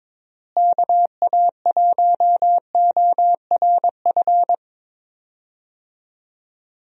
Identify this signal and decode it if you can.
Morse code, heard as a keyed tone, about 22 wpm, 700 Hz. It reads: KA1ORF
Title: USA Call Signs - 6 Characters - Repeated 3x - 22wpm